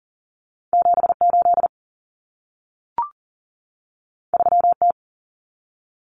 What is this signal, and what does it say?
Morse code, heard as a keyed tone, about 40 wpm, 700 Hz. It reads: 78 3T
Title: Common Cut Numbers - 2-Digit 40wpm